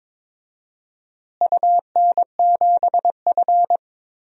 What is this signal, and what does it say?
Morse code, heard as a keyed tone, about 21 wpm, 700 Hz. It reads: UN7F
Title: DXCC-100 Call Signs - 4 Characters - Repeated 2x - 22wpm